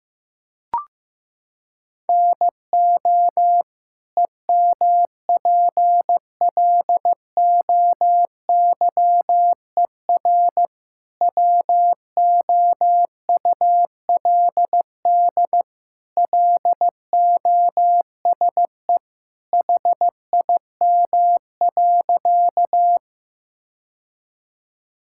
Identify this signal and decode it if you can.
Morse code, heard as a keyed tone, about 15 wpm, 700 Hz. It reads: NO EMPLOYER WOULD LOSE HIM.
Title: Sentences from Top 1600 Words 15wpm